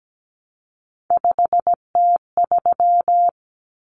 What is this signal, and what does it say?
Morse code, heard as a keyed tone, about 17 wpm, 700 Hz. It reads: 5T3
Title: Common Cut Numbers - 3-Digit 17wpm